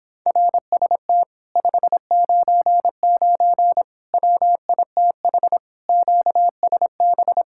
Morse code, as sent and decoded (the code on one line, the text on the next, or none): .-. ... - / ..... ----. ----. / .-- .. - .... / --.- ... -...
RST 599 WITH QSB